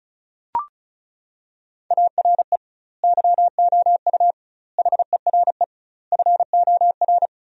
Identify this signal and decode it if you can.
Morse code, heard as a keyed tone, about 34 wpm, 700 Hz. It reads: ARE YOU HERE FOR
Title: 2x Word-Spacing - Sentences from Top 200 Words - 35wpm